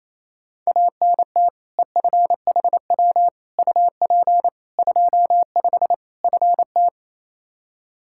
Morse code, heard as a keyed tone, about 27 wpm, 700 Hz. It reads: ANT EFHW UP 25 FT
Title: K1USN SST QSO Practice 28wpm